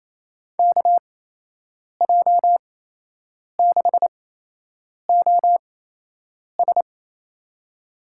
Morse code, heard as a keyed tone, about 28 wpm, spaced as for 8 wpm, 700 Hz. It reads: KJ6OS
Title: USA Call Signs - 5 Characters - 8x Character Spacing - 28wpm